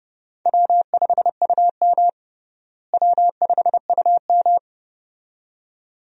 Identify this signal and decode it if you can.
Morse code, heard as a keyed tone, about 30 wpm, 700 Hz. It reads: W5UM W5UM
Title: USA Call Signs - 4 Characters - Repeated 2x - 30wpm